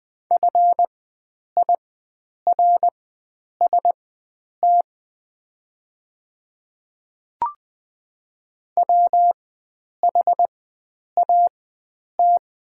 Morse code, heard as a keyed tone, about 20 wpm, spaced as for 9 wpm, 700 Hz. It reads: FIRST WHAT
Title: 4x Character Spacing - Top 100 Words - 20wpm